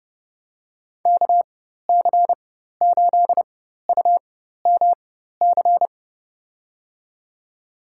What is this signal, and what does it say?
Morse code, heard as a keyed tone, about 30 wpm, spaced as for 14 wpm, 700 Hz. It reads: KC8UMC
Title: USA Call Signs - 6 Characters - 4x Character Spacing - 30wpm